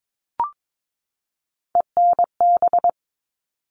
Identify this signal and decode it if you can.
Morse code, heard as a keyed tone, about 22 wpm, 700 Hz. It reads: ENB
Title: All Cut Numbers - 3-Digit 22wpm